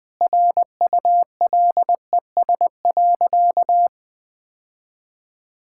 Morse code, heard as a keyed tone, about 20 wpm, 700 Hz. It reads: RULES.
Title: Sentences with 6-Word Phrases - Spoken Beginning - 20wpm